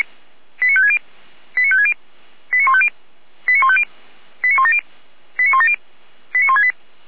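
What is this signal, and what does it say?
Five-tone selective call (ZVEI), heard as quick runs of five tones, tones of about 70 ms. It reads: 89560 89560 89150 89150 89170 89170 89168
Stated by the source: ZVEI five-tone selective call 89167, received by SDR, feuerwehr-orlamuende.org fire brigade.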